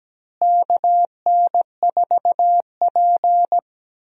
Morse code, heard as a keyed tone, about 17 wpm, 700 Hz. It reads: KN4P
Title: USA Call Signs - 4 Characters - Repeated 3x - 17wpm